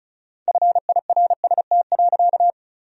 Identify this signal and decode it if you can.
Morse code, heard as a keyed tone, about 36 wpm, 700 Hz. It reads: FIRST.
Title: 4x Word-Spacing - Sentences from Top 300 Words - 35wpm